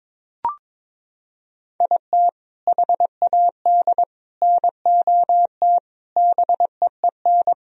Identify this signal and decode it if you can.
Morse code, heard as a keyed tone, about 21 wpm, 700 Hz. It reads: IT HAD NOT BEEN
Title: Sentences from Top 1000 Words - Review Part 1 - 22wpm